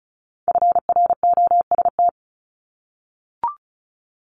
Morse code, heard as a keyed tone, about 36 wpm, 700 Hz. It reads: FROST
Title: Multi-letter Phonemes 35wpm